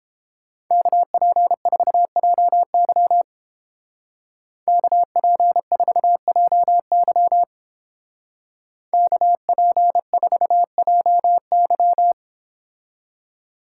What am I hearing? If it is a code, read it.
Morse code, heard as a keyed tone, about 29 wpm, 700 Hz. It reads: KP4JY KP4JY KP4JY